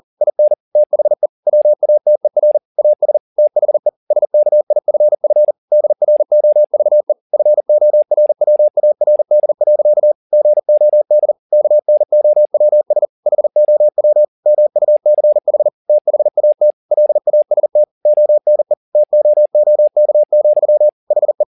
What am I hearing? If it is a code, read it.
Morse code, heard as a keyed tone, about 40 wpm, 600 Hz. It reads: IN THE WATER AS THE SKIFF DROVE FORWARD. GOD KNOWS HOW MUCH THAT LAST ONE TOOK, HE
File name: the old man and the sea 2 40wpm